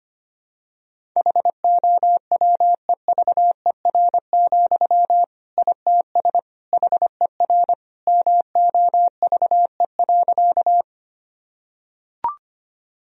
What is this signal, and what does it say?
Morse code, heard as a keyed tone, about 25 wpm, 700 Hz. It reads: HOWEVER, ITS HER MOVE.
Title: Sentences from Top 1000 Words - Review Part 1 - 25wpm